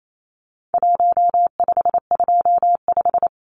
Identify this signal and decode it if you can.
Morse code, heard as a keyed tone, about 28 wpm, 700 Hz. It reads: 1525
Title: Numbers - Contest Style - 4-Digit 28wpm